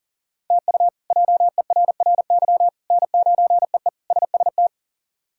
Morse code, heard as a keyed tone, about 40 wpm, 700 Hz. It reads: TU JERRY N9EE SST
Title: K1USN SST QSO Practice - 40wpm